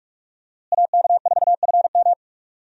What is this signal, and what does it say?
Morse code, heard as a keyed tone, about 45 wpm, 700 Hz. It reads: AK4FM
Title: USA Call Signs - 5 Characters - Repeated 2x - 45wpm